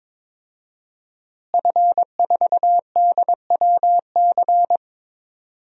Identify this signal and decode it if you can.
Morse code, heard as a keyed tone, about 22 wpm, 700 Hz. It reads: F4DWC
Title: DXCC-100 Call Signs - 5 Characters - Repeated 2x - 22wpm